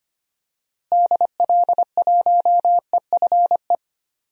Morse code, heard as a keyed tone, about 25 wpm, 700 Hz. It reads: DL1EFE
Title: DXCC-100 Call Signs - 6 Characters - Repeated 2x - 25wpm